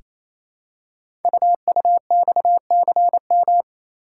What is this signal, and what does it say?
Morse code, heard as a keyed tone, about 27 wpm, 700 Hz. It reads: UUXCM
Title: Code Groups - 5 Letters - 28wpm